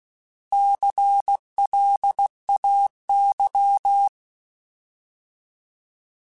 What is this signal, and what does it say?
Morse code, heard as a keyed tone, about 16 wpm, 800 Hz. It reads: CLAY
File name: Alphabetical / Alphabetical mp3s / Cs @15